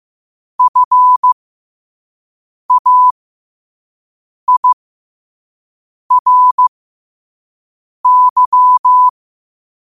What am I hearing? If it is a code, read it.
Morse code, heard as a keyed tone, about 15 wpm, spaced as for 5 wpm, 1000 Hz. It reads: FAIRY
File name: qotdfiles / 5wpm